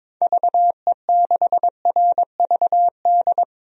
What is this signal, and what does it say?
Morse code, heard as a keyed tone, about 22 wpm, 700 Hz. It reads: VE6RVD
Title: DXCC-100 Call Signs - 6 Characters - Repeated 3x - 22wpm